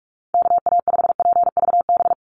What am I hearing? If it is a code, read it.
Morse code, heard as a keyed tone, about 44 wpm, 690 Hz. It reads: KA5PVB
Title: NA SOTA Chaser Callsigns 45wpm